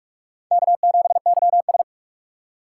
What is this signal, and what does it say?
Morse code, heard as a keyed tone, about 46 wpm, 700 Hz. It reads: K7YS
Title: USA Call Signs - 4 Characters - 6x Character Spacing - 45wpm